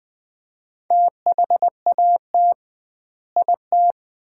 Morse code, heard as a keyed tone, about 20 wpm, 700 Hz. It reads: THAT IT
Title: 2x Word-Spacing - Sets of 4 Words - 20wpm